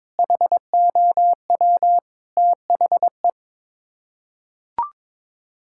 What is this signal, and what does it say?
Morse code, heard as a keyed tone, about 22 wpm, 700 Hz. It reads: HOW THE
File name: practice / 22wpm / Sets of 8 Words - 22wpm